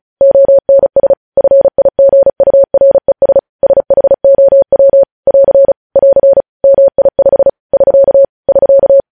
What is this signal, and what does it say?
Morse code, heard as a keyed tone, about 36 wpm, 550 Hz. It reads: ONS FIGURES SHOW <AR> <AR> MI5 <SK> <SK>